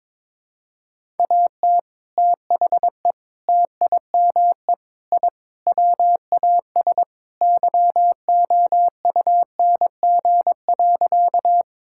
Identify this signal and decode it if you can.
Morse code, heard as a keyed tone, about 22 wpm, 700 Hz. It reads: AT THE TIME I WAS YOUNG.
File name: practice / 22wpm / Sentences with 3-Word Phrases - 22wpm